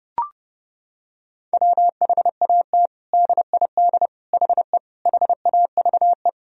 Morse code, heard as a keyed tone, about 30 wpm, 700 Hz. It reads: WHAT DID HE HAVE
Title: Sentences from Top 1000 Words - Review Part 4 - 30wpm